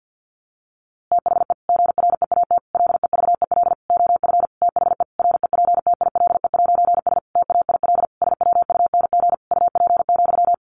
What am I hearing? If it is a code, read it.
Morse code, heard as a keyed tone, about 49 wpm, 700 Hz. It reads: THE GREAT LEVEL OF THE REPTILE'S TAIL SWUNG UP,
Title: A Sound Of Thunder 50wpm